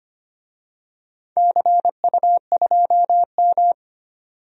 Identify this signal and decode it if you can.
Morse code, heard as a keyed tone, about 25 wpm, 700 Hz. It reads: CU2M
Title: DXCC-100 Call Signs - 4 Characters - 25wpm